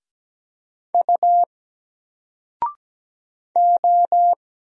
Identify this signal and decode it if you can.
Morse code, heard as a keyed tone, about 17 wpm, 700 Hz. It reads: U O
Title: Single Letter-Number 17wpm